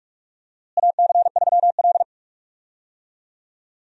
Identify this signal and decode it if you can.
Morse code, heard as a keyed tone, about 41 wpm, 700 Hz. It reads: AC3L
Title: USA Call Signs - 4 Characters - Repeated 3x - 45wpm